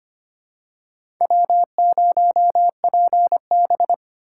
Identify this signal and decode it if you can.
Morse code, heard as a keyed tone, about 25 wpm, 700 Hz. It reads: W0PB
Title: USA Call Signs - 3x Character Spacing - 25wpm